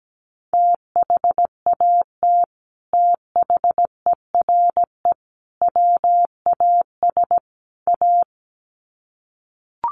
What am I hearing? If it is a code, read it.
Morse code, heard as a keyed tone, about 17 wpm, 700 Hz. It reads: THAT THERE WAS A
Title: Sets of 4 Words 17wpm